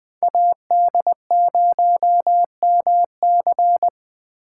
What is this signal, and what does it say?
Morse code, heard as a keyed tone, about 20 wpm, 700 Hz. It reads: AD0MC